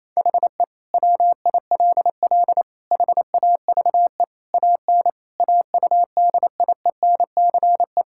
Morse code, heard as a keyed tone, about 27 wpm, 700 Hz. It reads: HE WILL HAVE AN AUDIENCE
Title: Sentences from Top 1000 Words - Review Part 5 - 28wpm